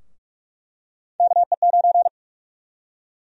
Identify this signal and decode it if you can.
Morse code, heard as a keyed tone, about 46 wpm, 700 Hz. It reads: KE9
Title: USA Call Sign Prefixes - Repeated 2x - 45wpm